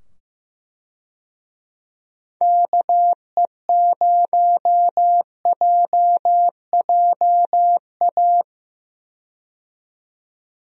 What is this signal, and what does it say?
Morse code, heard as a keyed tone, about 15 wpm, 700 Hz. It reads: KE0JJA
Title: USA Call Signs - 6 Characters - 15wpm